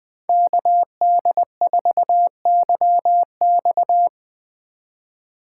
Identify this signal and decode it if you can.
Morse code, heard as a keyed tone, about 20 wpm, 700 Hz. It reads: KD4YX
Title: Speed Racing - USA 2x2 Call Signs 17wpm